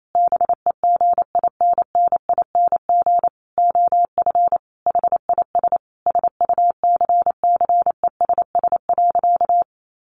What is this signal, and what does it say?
Morse code, heard as a keyed tone, about 28 wpm, 700 Hz. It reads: BEGINNING OF HIS SUCCESS.
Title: Sentences from Top 900 Words 28wpm